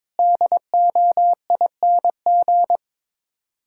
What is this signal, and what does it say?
Morse code, heard as a keyed tone, about 21 wpm, 700 Hz. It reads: DOING
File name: practice / 22wpm / Sets of 8 Words - 22wpm